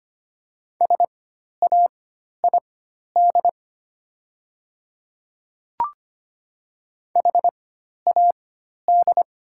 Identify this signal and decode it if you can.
Morse code, heard as a keyed tone, about 25 wpm, spaced as for 12 wpm, 700 Hz. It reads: SAID HAD
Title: 4x Character Spacing - Top 100 Words - 25wpm